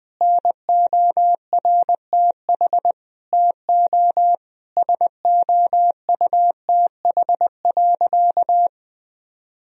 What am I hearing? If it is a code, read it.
Morse code, heard as a keyed tone, about 20 wpm, 700 Hz. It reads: NORTH TO SOUTH.
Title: Sentences from Top 1000 Words - Review Part 2 - 20wpm